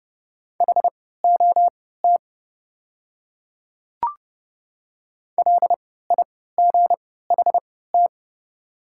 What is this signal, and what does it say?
Morse code, heard as a keyed tone, about 30 wpm, spaced as for 17 wpm, 690 Hz. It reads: HOT LIGHT
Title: 3x Character Spacing - Top 200 Words - 30wpm